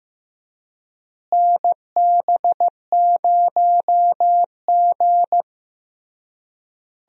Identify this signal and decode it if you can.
Morse code, heard as a keyed tone, about 16 wpm, 700 Hz. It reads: NB0G
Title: USA Call Signs - 4 Characters - Repeated 3x - 15wpm